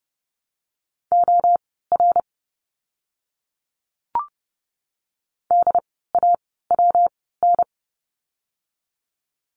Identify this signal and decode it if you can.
Morse code, heard as a keyed tone, about 30 wpm, spaced as for 17 wpm, 700 Hz. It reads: OR DAWN
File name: course / 30:10wpm / 115 Lesson 19f - Letter w - Words 30wpm